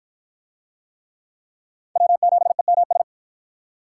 Morse code, heard as a keyed tone, about 53 wpm, 700 Hz. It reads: W7EMS